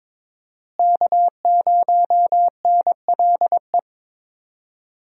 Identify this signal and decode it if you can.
Morse code, heard as a keyed tone, about 22 wpm, 700 Hz. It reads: K0NLE
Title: USA Call Signs - 5 Characters - Repeated 3x - 22wpm